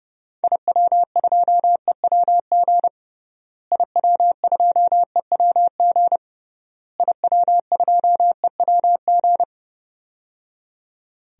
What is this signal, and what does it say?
Morse code, heard as a keyed tone, about 30 wpm, 700 Hz. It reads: IW2EWG IW2EWG IW2EWG
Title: DXCC-100 Call Signs - 6 Characters - Repeated 3x - 30wpm